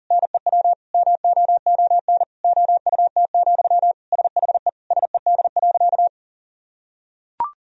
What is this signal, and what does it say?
Morse code, heard as a keyed tone, about 37 wpm, 700 Hz. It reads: NEW MOON OUT, SHE SED.
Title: Sentences from Top 500 Words and Abbreviations 40wpm